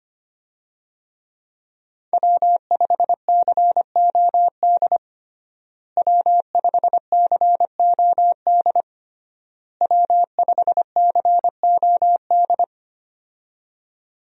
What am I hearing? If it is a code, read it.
Morse code, heard as a keyed tone, about 25 wpm, 700 Hz. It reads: W5COD W5COD W5COD
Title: USA Call Signs - 5 Characters - Repeated 3x - 25wpm